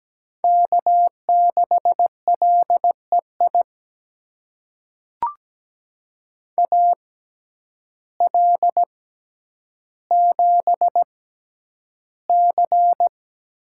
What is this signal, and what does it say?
Morse code, heard as a keyed tone, about 17 wpm, spaced as for 9 wpm, 700 Hz. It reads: K6LEI AL7C
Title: USA Call Signs - 5 Characters - 6x Character Spacing - 17wpm